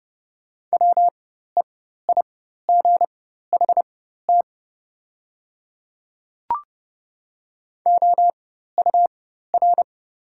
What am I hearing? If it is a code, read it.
Morse code, heard as a keyed tone, about 30 wpm, spaced as for 14 wpm, 690 Hz. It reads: WEIGHT OUR
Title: 4x Character Spacing - Top 500 Words - 30wpm